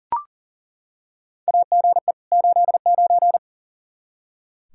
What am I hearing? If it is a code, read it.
Morse code, heard as a keyed tone, about 40 wpm, 700 Hz. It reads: AGE 89
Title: Practice QSO - Exchange 3 - More Info - 40wpm